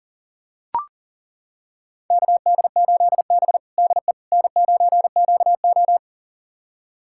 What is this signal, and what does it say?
Morse code, heard as a keyed tone, about 37 wpm, 700 Hz. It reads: KD8B DE N9QO
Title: USA Callsign DE Callsign - 4 Characters - 40wpm